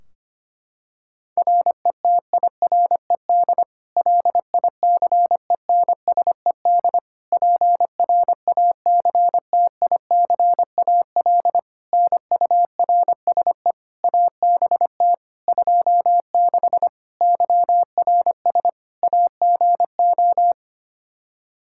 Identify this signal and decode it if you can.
Morse code, heard as a keyed tone, about 25 wpm, 700 Hz. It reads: RETIRED LICENSED PRACTICAL NURSE ABT 26 YRS AGO
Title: Practice QSO - Exchange 3 - More Info - 25wpm